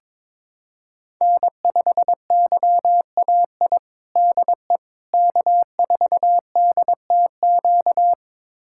Morse code, heard as a keyed tone, about 21 wpm, 700 Hz. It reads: N5YAI DE K4DTQ
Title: USA Callsign DE Callsign 22wpm